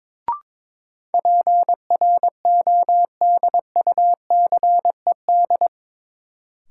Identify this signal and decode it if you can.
Morse code, heard as a keyed tone, about 22 wpm, 700 Hz. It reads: PRODUCED
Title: Sentences from Top 1000 Words - Review Part 6 - New Word Drill - 22wpm